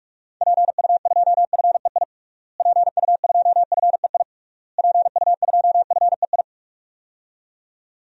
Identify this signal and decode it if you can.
Morse code, heard as a keyed tone, about 46 wpm, 700 Hz. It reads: PU2FEI PU2FEI PU2FEI
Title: DXCC-100 Call Signs - 6 Characters - Repeated 3x - 45wpm